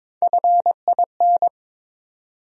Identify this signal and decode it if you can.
Morse code, heard as a keyed tone, about 21 wpm, 700 Hz. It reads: FIN
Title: USA Call Sign Postfixes - 3 Letters 22wpm